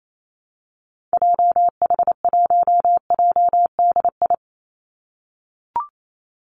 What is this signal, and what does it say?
Morse code, heard as a keyed tone, about 28 wpm, 700 Hz. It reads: JH1JDI
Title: Call Signs - 28wpm